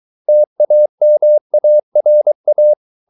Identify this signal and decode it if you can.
Morse code, heard as a keyed tone, about 23 wpm, 600 Hz. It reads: TAMARA